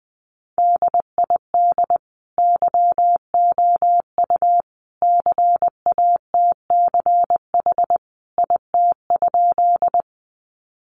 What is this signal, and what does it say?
Morse code, heard as a keyed tone, about 20 wpm, 700 Hz. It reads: DID YOU CATCH IT?
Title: Sentences from Top 1500 Words 20wpm